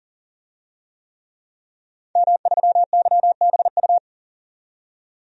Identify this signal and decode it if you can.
Morse code, heard as a keyed tone, about 40 wpm, 700 Hz. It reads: M3YBU
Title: DXCC-100 Call Signs - 5 Characters - 40wpm